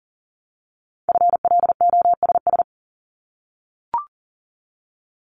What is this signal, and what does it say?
Morse code, heard as a keyed tone, about 38 wpm, 700 Hz. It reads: FLOSS
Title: Multi-letter Phonemes Repeated 3x 40wpm